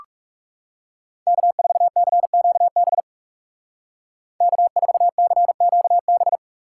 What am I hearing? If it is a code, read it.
Morse code, heard as a keyed tone, about 38 wpm, 700 Hz. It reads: K4CQB K4CQB